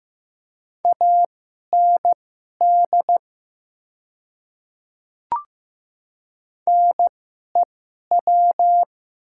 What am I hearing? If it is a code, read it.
Morse code, heard as a keyed tone, about 15 wpm, spaced as for 11 wpm, 700 Hz. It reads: AND NEW
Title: 2x Character Spacing - Top 200 Words - 15wpm